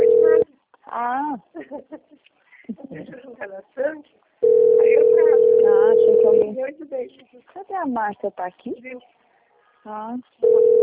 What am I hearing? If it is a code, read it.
Ringback tone (North America), 440 + 480 Hz, with speech.
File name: rg-1600-unknown- - - .35599